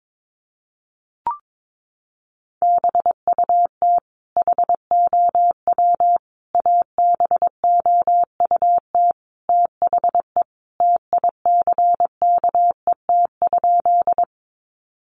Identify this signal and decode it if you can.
Morse code, heard as a keyed tone, about 22 wpm, 700 Hz. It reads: BUT HOW ABOUT THE TICKET?
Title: Sentences from Top 1400 Words 22wpm